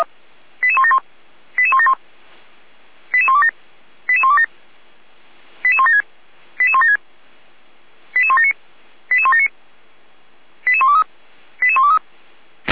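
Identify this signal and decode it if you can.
Five-tone selective call (ZVEI), heard as quick runs of five tones, tones of about 70 ms. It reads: 88171 88171 88127 88127 88176 88176 88179 88179 88123 88123
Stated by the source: ZVEI five-tone selective call received by SDR, feuerwehr-orlamuende fire brigade (f85360).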